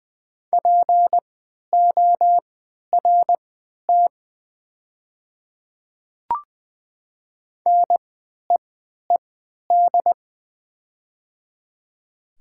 Morse code, heard as a keyed tone, about 20 wpm, spaced as for 11 wpm, 700 Hz. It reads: PORT NEED
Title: 3x Character Spacing - Top 200 Words - 20wpm